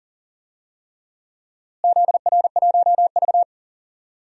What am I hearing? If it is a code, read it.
Morse code, heard as a keyed tone, about 40 wpm, 700 Hz. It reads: ZR1V
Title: DXCC-100 Call Signs - 4 Characters - Repeated 3x - 40wpm